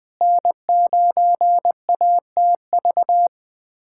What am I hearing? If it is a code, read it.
Morse code, heard as a keyed tone, about 20 wpm, 700 Hz. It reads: N9ATV
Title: USA Callsign DE Callsign - 5 Characters - 20wpm